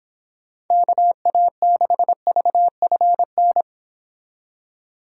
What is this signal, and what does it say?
Morse code, heard as a keyed tone, about 26 wpm, 700 Hz. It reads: KA6VFN